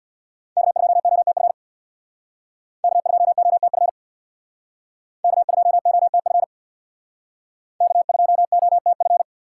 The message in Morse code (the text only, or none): K2OTF K2OTF K2OTF K2OTF